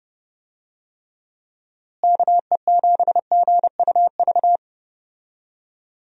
Morse code, heard as a keyed tone, about 31 wpm, 700 Hz. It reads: KE7GUV
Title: USA Call Signs - 6 Characters - Repeated 2x - 30wpm